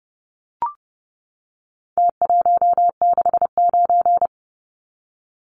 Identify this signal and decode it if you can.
Morse code, heard as a keyed tone, about 30 wpm, 700 Hz. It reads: T169
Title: Numbers - Contest Style - 4-Digit 30wpm